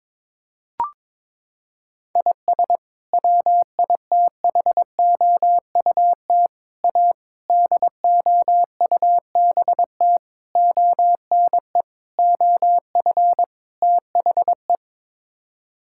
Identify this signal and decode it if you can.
Morse code, heard as a keyed tone, about 21 wpm, 700 Hz. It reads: IS WITHOUT A DOUBT ONE OF THE
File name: practice / 22wpm / Sets of 7 Words - 22wpm